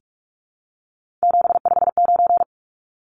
Morse code, heard as a keyed tone, about 44 wpm, 700 Hz. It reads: 759
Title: Numbers - 3-Digit 45wpm